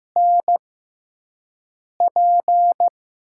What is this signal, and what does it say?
Morse code, heard as a keyed tone, about 15 wpm, spaced as for 5 wpm, 700 Hz. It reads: NP
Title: USA Call Signs - 6 Characters - 6x Character Spacing - 15wpm